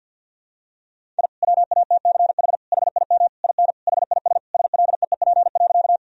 Morse code, heard as a keyed tone, about 48 wpm, 700 Hz. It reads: I WATCH HIM IN HIS SLEEP.
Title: Sentences from Top 1000 Words - Review Part 1 - 50wpm